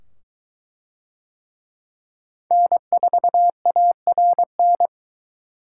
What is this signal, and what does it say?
Morse code, heard as a keyed tone, about 21 wpm, 700 Hz. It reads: N4ARN